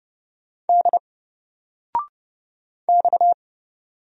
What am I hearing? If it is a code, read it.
Morse code, heard as a keyed tone, about 30 wpm, 700 Hz. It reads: D X
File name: Single-Letter-Number-30wpm